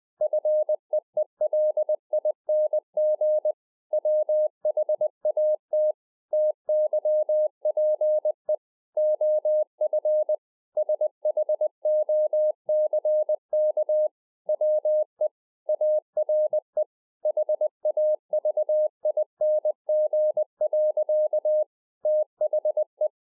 Morse code, heard as a keyed tone, about 20 wpm, 610 Hz. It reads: FEELING WHAT TYPE OF SHOCK WE ARE HAVING. THE